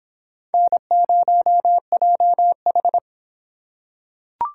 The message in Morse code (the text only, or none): N0JH